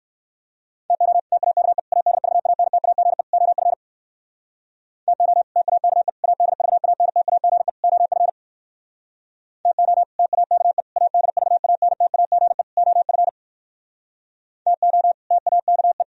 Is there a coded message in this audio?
Yes: Morse code from 0.9 to 16.0 s, heard as a keyed tone, about 57 wpm, 700 Hz. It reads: TO TAKE ADVANTAGE OF TO TAKE ADVANTAGE OF TO TAKE ADVANTAGE OF TO TAKE